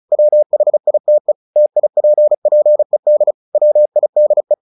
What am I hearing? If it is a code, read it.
Morse code, heard as a keyed tone, about 34 wpm, 600 Hz. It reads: WHITE TIPPED WIDE
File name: the old man and the sea 2 35wpm